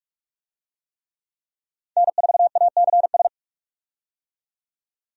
Morse code, heard as a keyed tone, about 46 wpm, 700 Hz. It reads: N4ACS
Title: USA Call Signs - 5 Characters - 45wpm